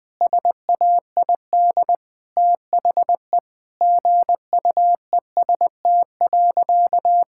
Morse code, heard as a keyed tone, about 21 wpm, 700 Hz. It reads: SAID THE GUEST.
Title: Sentences from Top 1600 Words 20wpm